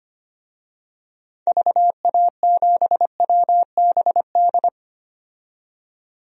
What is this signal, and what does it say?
Morse code, heard as a keyed tone, about 25 wpm, 700 Hz. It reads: VA7WBD
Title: DXCC-100 Call Signs - 6 Characters - 25wpm